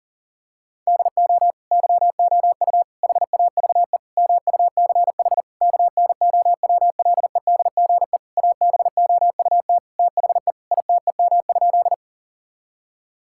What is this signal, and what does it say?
Morse code, heard as a keyed tone, about 40 wpm, 700 Hz. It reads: DO YOU HAVE MUCH KNOWLEDGE ABOUT THE ITEM?
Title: Sentences from Top 1000 Words - Review Part 6 - 40wpm